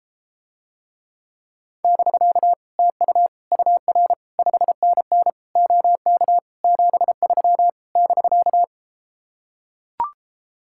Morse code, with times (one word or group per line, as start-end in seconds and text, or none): 1.8-2.5 <BK>
2.8-3.3 TU
3.5-4.1 UR
4.4-5.3 5NN
5.6-6.4 OK
6.6-7.7 73
8.0-8.6 <BK>